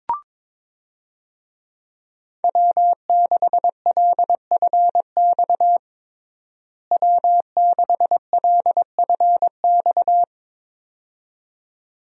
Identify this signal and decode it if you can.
Morse code, heard as a keyed tone, about 22 wpm, 700 Hz. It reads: W6LFX W6LFX
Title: USA Call Signs - 5 Characters - Repeated 2x - 22wpm